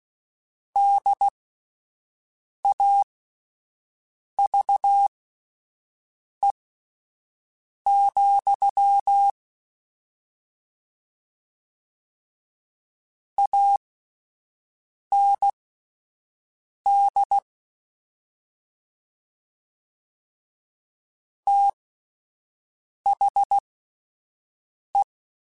Morse code, heard as a keyed tone, about 16 wpm, spaced as for 5 wpm, 800 Hz. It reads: DAVE, AND THE
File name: (spaced) / QSOs Spaced-mp3s / kc7haa @10 sp